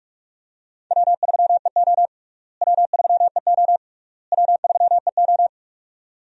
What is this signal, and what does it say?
Morse code, heard as a keyed tone, about 42 wpm, 700 Hz. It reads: W3EO W3EO W3EO
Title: USA Call Signs - 4 Characters - Repeated 3x - 45wpm